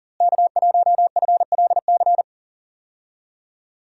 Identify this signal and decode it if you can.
Morse code, heard as a keyed tone, about 40 wpm, 700 Hz. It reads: K1FLC
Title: USA Call Signs - 5 Characters - Repeated 2x - 40wpm